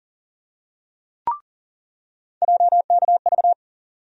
Morse code, heard as a keyed tone, about 40 wpm, 700 Hz. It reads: JKV
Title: USA Call Sign Postfixes - 3 Letters 40wpm